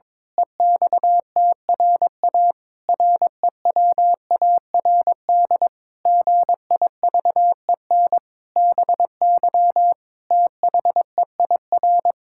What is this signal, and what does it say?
Morse code, heard as a keyed tone, about 21 wpm, 700 Hz. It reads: EXTRA REWARD GIVEN BY THEIR
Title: Pangram Challenge v2 - 22wpm